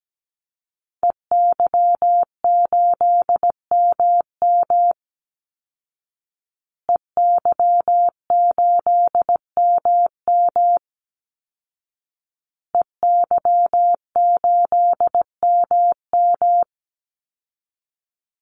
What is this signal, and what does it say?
Morse code, heard as a keyed tone, about 17 wpm, 700 Hz. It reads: EY8MM EY8MM EY8MM
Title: Call Signs Repeated 2x - 17wpm